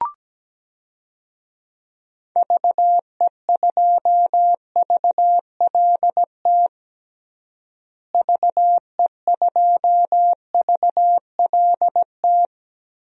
Morse code, heard as a keyed tone, about 17 wpm, 700 Hz. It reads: VE2VLT VE2VLT
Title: DXCC-100 Call Signs - 6 Characters - Repeated 2x - 17wpm